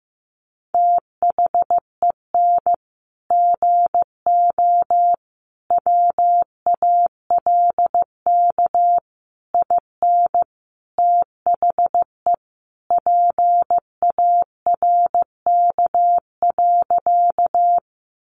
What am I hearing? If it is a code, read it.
Morse code, heard as a keyed tone, about 15 wpm, 700 Hz. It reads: THEN GO WALK IN THE PARK.
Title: Sentences from Top 1300 Words 15wpm